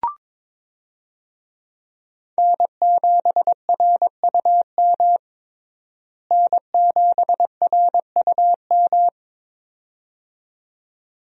Morse code, heard as a keyed tone, about 22 wpm, 700 Hz. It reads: N7RUM N7RUM
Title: USA Call Signs - 5 Characters - Repeated 2x - 22wpm